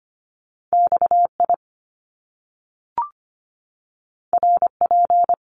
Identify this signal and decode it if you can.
Morse code, heard as a keyed tone, about 25 wpm, 700 Hz. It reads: XI RP
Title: Top Two Letter Combinations in Words 25wpm